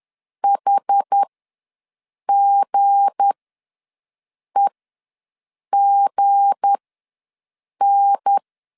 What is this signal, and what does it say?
Morse code, heard as a keyed tone, about 10 wpm, spaced as for 6 wpm, 800 Hz. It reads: HGEGN
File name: LEZIONI / 7 GF9Z / 7 G ws 10wpm 3843